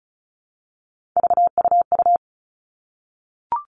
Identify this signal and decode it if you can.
Morse code, heard as a keyed tone, about 34 wpm, 700 Hz. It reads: VUU
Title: All Cut Numbers - 3-Digit 35wpm